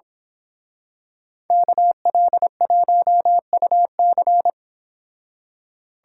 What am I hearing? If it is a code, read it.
Morse code, heard as a keyed tone, about 27 wpm, 700 Hz. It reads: KL1UC
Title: Speed Racing - USA 2x2 Call Signs 22wpm